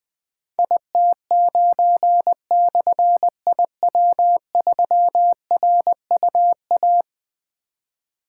Morse code, heard as a keyed tone, about 20 wpm, 700 Hz. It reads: IT9/IW3RUA
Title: Difficult Call Signs 20wpm